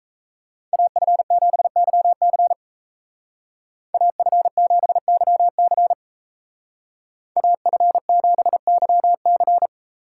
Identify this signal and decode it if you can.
Morse code, heard as a keyed tone, about 33 wpm, 700 Hz. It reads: AF7YC AF7YC AF7YC
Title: Speed Racing - USA 2x2 Call Signs 28wpm